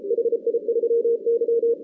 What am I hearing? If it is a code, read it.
Morse code, heard as a keyed tone, about 33 wpm, 470 Hz. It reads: HI3Y
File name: 2016 arrl dx cw / 2016-ARRL-DX-CW-K5TR- - 0-41